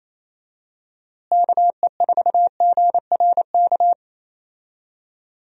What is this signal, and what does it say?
Morse code, heard as a keyed tone, about 28 wpm, 700 Hz. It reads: KE4GRK
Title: USA Call Signs - 6 Characters - Repeated 2x - 28wpm